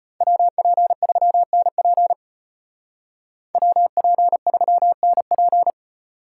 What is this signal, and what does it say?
Morse code, heard as a keyed tone, about 31 wpm, 700 Hz. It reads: WP3NP WP3NP